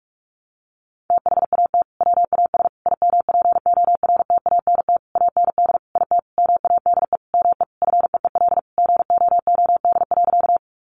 Morse code, heard as a keyed tone, about 44 wpm, 700 Hz. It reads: THAT WAS IMPORTANT AND IT MADE ME FEEL GOOD.
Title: Sentences from Top 600 Words 45wpm